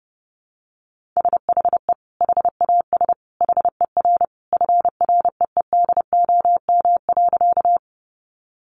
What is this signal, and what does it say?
Morse code, heard as a keyed tone, about 30 wpm, 700 Hz. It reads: SHE HAS HER FREEDOM.
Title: Sentences from Top 800 Words 30wpm